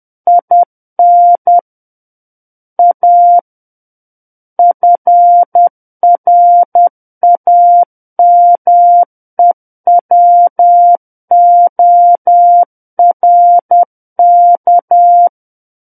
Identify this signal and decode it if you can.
Morse code, heard as a keyed tone, about 10 wpm, 700 Hz. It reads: IN A FRAMEWORK
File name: Book One / 01-Chapter 1 - 10 WPM